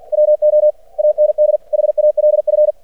Morse code, heard as a keyed tone, about 50 wpm, 600 Hz. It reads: WO AND HALF
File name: archived / CRSnet